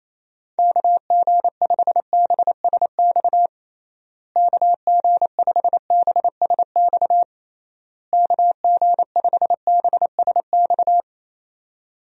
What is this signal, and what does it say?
Morse code, heard as a keyed tone, about 28 wpm, 700 Hz. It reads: KG5BSX KG5BSX KG5BSX
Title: USA Call Signs - 6 Characters - Repeated 3x - 28wpm